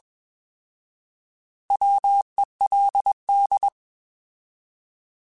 Morse code, heard as a keyed tone, about 21 wpm, 800 Hz. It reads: WELD
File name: Alphabetical / Alphabetical mp3s / Ws @20